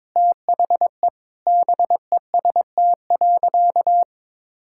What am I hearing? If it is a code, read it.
Morse code, heard as a keyed tone, about 22 wpm, 700 Hz. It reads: THE BEST.
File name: practice / 22wpm / Sentences with 3-Word Phrases - 22wpm